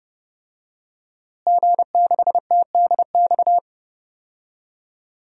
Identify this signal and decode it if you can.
Morse code, heard as a keyed tone, about 30 wpm, 700 Hz. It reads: G6TDX
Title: DXCC-100 Call Signs - 5 Characters - Repeated 2x - 30wpm